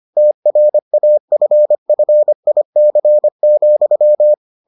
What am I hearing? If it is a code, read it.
Morse code, heard as a keyed tone, about 25 wpm, 600 Hz. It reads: TRAFFIC,